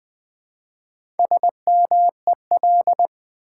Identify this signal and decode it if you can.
Morse code, heard as a keyed tone, about 20 wpm, 700 Hz. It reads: SMEL
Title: Build A Word - 1 Letter At A Time - 3 to 5-Letter Words - 20wpm